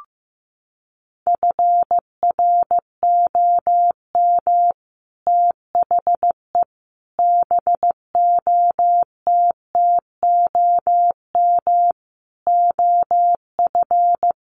Morse code, heard as a keyed tone, about 15 wpm, 700 Hz. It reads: FROM THE BOTTOM OF
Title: Sets of 5 Words Encore2 15wpm